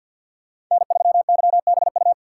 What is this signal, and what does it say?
Morse code, heard as a keyed tone, about 53 wpm, 690 Hz. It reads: N3YBU
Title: USA Call Signs - 5 Characters - 6x Character Spacing - 50wpm